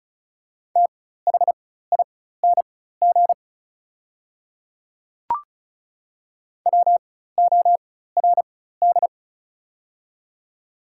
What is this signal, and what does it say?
Morse code, heard as a keyed tone, about 36 wpm, spaced as for 16 wpm, 690 Hz. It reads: THING WORD
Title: 4x Character Spacing - Top 100 Words - 35wpm